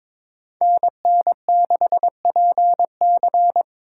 Morse code, heard as a keyed tone, about 22 wpm, 700 Hz. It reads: NN6PC
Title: USA Call Signs - 5 Characters - 4x Character Spacing - 22wpm